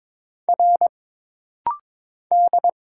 Morse code, heard as a keyed tone, about 22 wpm, 700 Hz. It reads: R D
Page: This rapid-fire timing is geared to push your ICR (Instant Character Recognition) skills.